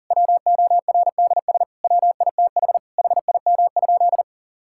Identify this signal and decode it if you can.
Morse code, heard as a keyed tone, about 40 wpm, 700 Hz. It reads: WORDS WITH HIM?
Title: Sentences from Top 1000 Words - Review Part 6 - 40wpm